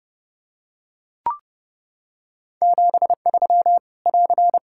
Morse code, heard as a keyed tone, about 30 wpm, 700 Hz. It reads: 73 <AR>
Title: Practice QSO - Exchange 4 - Ending - 30wpm